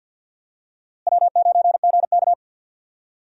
Morse code, heard as a keyed tone, about 53 wpm, 700 Hz. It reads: W9GK
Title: USA Call Signs - 4 Characters - Repeated 3x - 50wpm